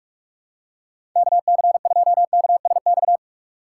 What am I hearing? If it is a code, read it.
Morse code, heard as a keyed tone, about 46 wpm, 700 Hz. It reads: KC2KSX
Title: USA Call Signs - 6 Characters - 2x Character Spacing - 45wpm